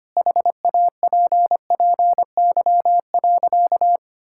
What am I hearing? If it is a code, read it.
Morse code, heard as a keyed tone, about 25 wpm, 700 Hz. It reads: HAPPY.
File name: practice / 25wpm / Sentences with 2-Word Phrases - 25wpm